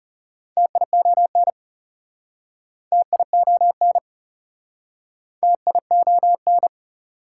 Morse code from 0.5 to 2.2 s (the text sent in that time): TION